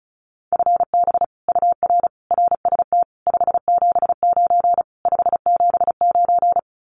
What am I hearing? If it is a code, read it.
Morse code, heard as a keyed tone, about 34 wpm, 700 Hz. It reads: FB UR RST 579 579
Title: Common QSO Phrases Encore 35wpm